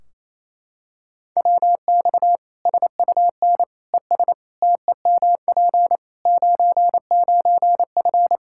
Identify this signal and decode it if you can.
Morse code, heard as a keyed tone, about 28 wpm, 700 Hz. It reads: WX SUN ES TEMP 99F
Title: Practice QSO - Exchange 2 - Info - 28wpm